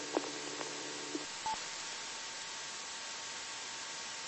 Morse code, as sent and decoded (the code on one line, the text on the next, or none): none